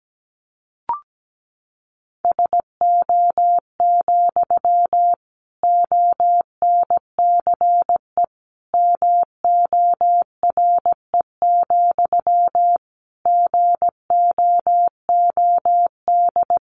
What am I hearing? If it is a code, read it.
Morse code, heard as a keyed tone, about 17 wpm, 700 Hz. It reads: SO, ONCE MORE, GOOD
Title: Sentences from Top 300 Words 17wpm